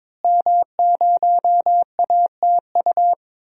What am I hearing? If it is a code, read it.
Morse code, heard as a keyed tone, about 22 wpm, 700 Hz. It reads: M0ATU
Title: DXCC-100 Call Signs - 5 Characters - Repeated 3x - 22wpm